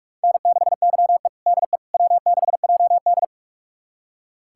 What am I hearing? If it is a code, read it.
Morse code, heard as a keyed tone, about 46 wpm, 700 Hz. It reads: N6YE DE W6JD
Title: USA Callsign DE Callsign - 4 Characters - 45wpm